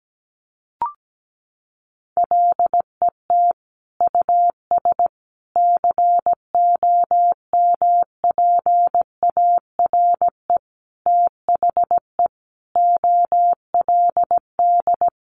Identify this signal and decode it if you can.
Morse code, heard as a keyed tone, about 17 wpm, 700 Hz. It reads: LET US COMPARE THE OLD
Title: Sentences from Top 1300 Words 17wpm